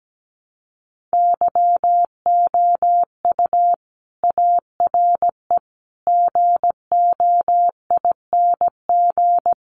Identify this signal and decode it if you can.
Morse code, heard as a keyed tone, about 17 wpm, 700 Hz. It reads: YOU ARE GOING
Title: Sets of 4 Words 17wpm